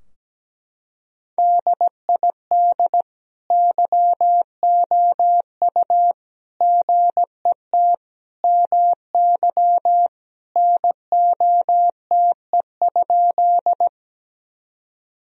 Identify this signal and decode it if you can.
Morse code, heard as a keyed tone, about 17 wpm, 700 Hz. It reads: DID YOU GET MY NOTE?
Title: Sentences from Top 1000 Words - Review Part 2 - 17wpm